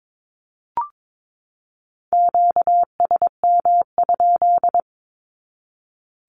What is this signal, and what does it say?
Morse code, heard as a keyed tone, about 22 wpm, 700 Hz. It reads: QSM?
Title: Q-Codes 22wpm